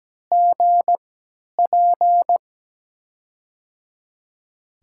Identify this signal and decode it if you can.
Morse code, heard as a keyed tone, about 17 wpm, spaced as for 10 wpm, 700 Hz. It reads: GP
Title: USA Call Signs - 6 Characters - 3x Character Spacing - 17wpm